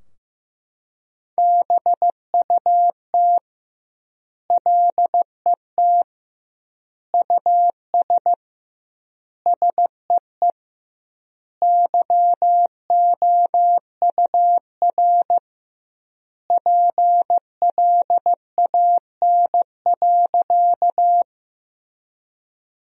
Morse code, heard as a keyed tone, about 16 wpm, 700 Hz. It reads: BUT LET US SEE YOUR PLAN.
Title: 2x Word-Spacing - Sentences from Top 500 Words - 15wpm